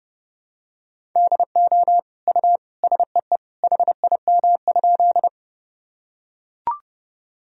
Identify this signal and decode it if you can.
Morse code, heard as a keyed tone, about 30 wpm, 700 Hz. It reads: DO U SEE HIM?
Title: Sentences from Top 100 Words and Abbreviations 30wpm